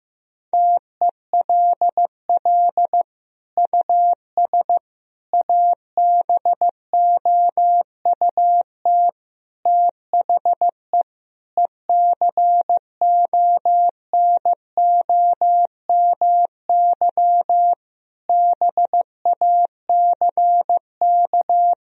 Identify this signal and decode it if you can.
Morse code, heard as a keyed tone, about 15 wpm, 700 Hz. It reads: TELL US ABOUT THE ECONOMY BACK
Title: Sentences from Top 1000 Words - Review Part 3 - 15wpm